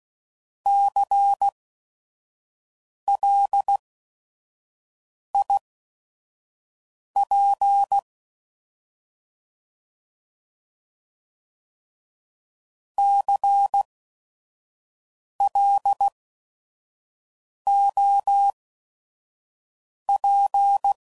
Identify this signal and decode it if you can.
Morse code, heard as a keyed tone, about 16 wpm, spaced as for 5 wpm, 800 Hz. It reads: CLIP CLOP
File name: Prefixes / By First Letter / Cs